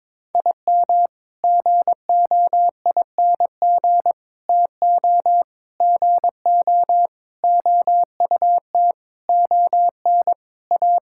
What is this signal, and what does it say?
Morse code, heard as a keyed tone, about 23 wpm, 700 Hz. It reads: IM GOING TO GO OUT ON A
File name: practice / 22wpm / Sentences with 8-Word Phrases - 22wpm